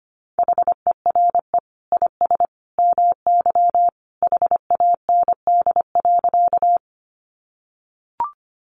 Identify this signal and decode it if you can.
Morse code, heard as a keyed tone, about 25 wpm, 700 Hz. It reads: HERE IS MY HAND.
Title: Sentences from Top 200 Words 25wpm